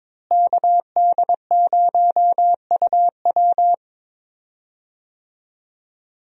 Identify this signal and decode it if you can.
Morse code, heard as a keyed tone, about 22 wpm, 700 Hz. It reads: KD0UW
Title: USA Call Signs - 5 Characters - Repeated 2x - 22wpm